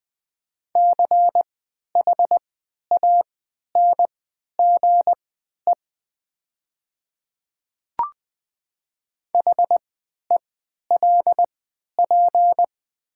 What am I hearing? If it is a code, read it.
Morse code, heard as a keyed tone, about 20 wpm, spaced as for 11 wpm, 700 Hz. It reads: CHANGE HELP
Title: 3x Character Spacing - Top 200 Words - 20wpm